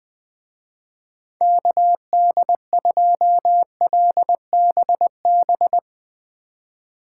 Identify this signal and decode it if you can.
Morse code, heard as a keyed tone, about 22 wpm, 700 Hz. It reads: KD2LBB
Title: Speed Racing - USA 2x3 Call Signs 20wpm